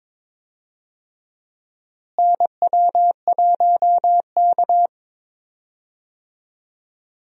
Morse code, heard as a keyed tone, about 22 wpm, 700 Hz. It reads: NW1K
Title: USA Call Signs - 4 Characters - Repeated 3x - 22wpm